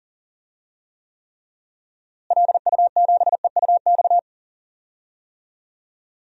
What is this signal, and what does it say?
Morse code, heard as a keyed tone, about 40 wpm, 700 Hz. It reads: LU7EUX
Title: DXCC-100 Call Signs - 6 Characters - Repeated 3x - 40wpm